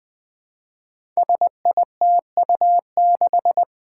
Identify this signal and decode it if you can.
Morse code, heard as a keyed tone, about 20 wpm, 700 Hz. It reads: SITU6
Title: Five Challenging Letter-Number Combinations 20wpm